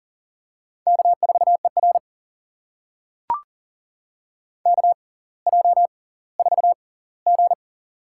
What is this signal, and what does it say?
Morse code, heard as a keyed tone, about 39 wpm, spaced as for 20 wpm, 690 Hz. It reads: K4ER KJ4G
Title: USA Call Signs - 4 Characters - 6x Character Spacing - 40wpm